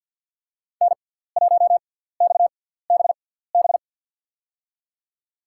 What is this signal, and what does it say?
Morse code, heard as a keyed tone, about 53 wpm, spaced as for 17 wpm, 700 Hz. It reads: N1XBB
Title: USA Call Signs - 5 Characters - 6x Character Spacing - 50wpm